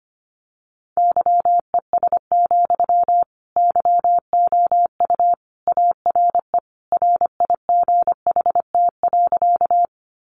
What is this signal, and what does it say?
Morse code, heard as a keyed tone, about 25 wpm, 700 Hz. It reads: YES, YOU ARE RIGHT.
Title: Sentences from Top 500 Words 25wpm